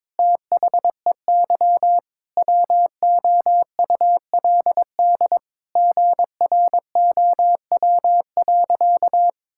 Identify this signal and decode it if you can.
Morse code, heard as a keyed tone, about 22 wpm, 700 Hz. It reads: THEY WOULD GROW.
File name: Sentences with 9-Word Phrases - 22wpm